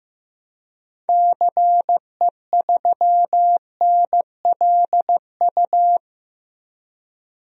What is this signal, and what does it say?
Morse code, heard as a keyed tone, about 15 wpm, 700 Hz. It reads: CE3NLU
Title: DXCC-100 Call Signs - 6 Characters - 15wpm